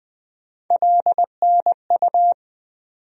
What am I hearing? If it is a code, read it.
Morse code, heard as a keyed tone, about 21 wpm, 700 Hz. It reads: LNU
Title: USA Call Sign Postfixes - 3 Letters 20wpm